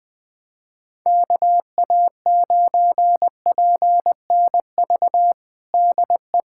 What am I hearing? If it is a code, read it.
Morse code, heard as a keyed tone, about 21 wpm, 700 Hz. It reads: KA9PNV DE
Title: USA Callsign DE Callsign 20wpm